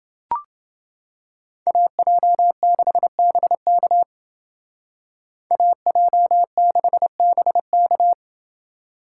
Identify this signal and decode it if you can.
Morse code, heard as a keyed tone, about 29 wpm, 700 Hz. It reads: AJ6BK AJ6BK